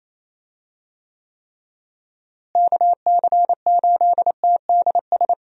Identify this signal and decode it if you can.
Morse code, heard as a keyed tone, about 27 wpm, 700 Hz. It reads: KC8TDS
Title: USA Call Signs - 6 Characters - Repeated 3x - 28wpm